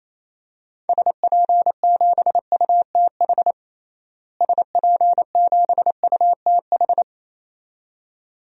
Morse code, heard as a keyed tone, about 28 wpm, 700 Hz. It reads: SP7UTH SP7UTH
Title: DXCC-100 Call Signs - 6 Characters - Repeated 2x - 28wpm